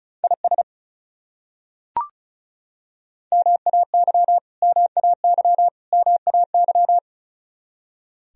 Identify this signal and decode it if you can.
Morse code, heard as a keyed tone, about 36 wpm, 700 Hz. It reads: IS MAY MAY MAY
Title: Top 100 Words Repeated 35wpm